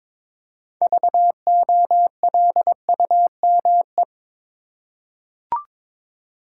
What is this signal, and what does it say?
Morse code, heard as a keyed tone, about 22 wpm, 700 Hz. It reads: VOLUME
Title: Six Letter Words - 22wpm